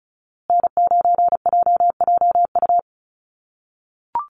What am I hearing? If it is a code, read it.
Morse code, heard as a keyed tone, about 34 wpm, 700 Hz. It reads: N9JJU
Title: USA Call Signs Repeated 3x 35wpm